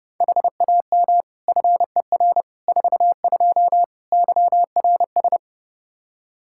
Morse code, heard as a keyed tone, about 30 wpm, 700 Hz. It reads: HAM FER 42 YRS
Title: Practice QSO - Exchange 3 - More Info - 30wpm